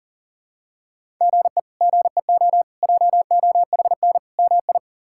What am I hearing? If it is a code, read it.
Morse code, heard as a keyed tone, about 40 wpm, 700 Hz. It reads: GE GEO JOHN MI
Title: K1USN SST QSO Practice - 40wpm